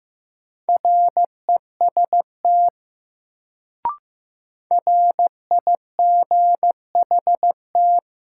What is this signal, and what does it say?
Morse code, heard as a keyed tone, about 15 wpm, 700 Hz. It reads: REST RIGHT
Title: Sets of 2 Words - New Word Drill - 15wpm